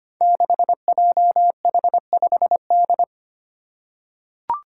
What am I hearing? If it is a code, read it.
Morse code, heard as a keyed tone, about 25 wpm, 700 Hz. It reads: 6JH5D
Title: Five Challenging Letter-Number Combinations 25wpm